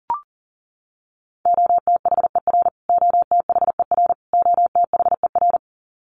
Morse code, heard as a keyed tone, about 37 wpm, 700 Hz. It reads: OTHER OTHER OTHER
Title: Top 300 Words Repeated 40wpm